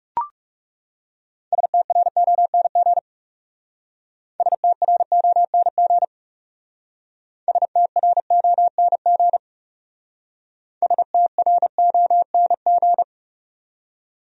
Morse code, heard as a keyed tone, about 37 wpm, 700 Hz. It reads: STRONG STRONG STRONG STRONG
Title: Speed-Racing Top 500 Words 30wpm